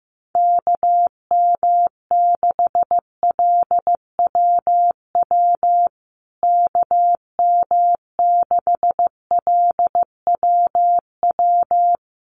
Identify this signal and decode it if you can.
Morse code, heard as a keyed tone, about 15 wpm, 700 Hz. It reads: KM6LWW KM6LWW
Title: USA Call Signs Repeated 2x 15wpm